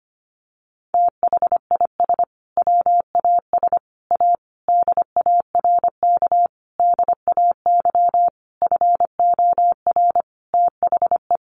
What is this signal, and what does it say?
Morse code, heard as a keyed tone, about 25 wpm, 700 Hz. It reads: THIS WAS A DARK DAY FOR THE
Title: Sentences from Top 1300 Words 25wpm